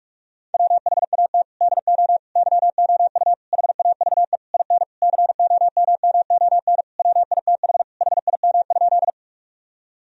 Morse code, heard as a keyed tone, about 43 wpm, 700 Hz. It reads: WHAT DO YOU HAVE IN COMMON WITH HIM?
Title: Sentences from Top 1000 Words - Review Part 3 - 45wpm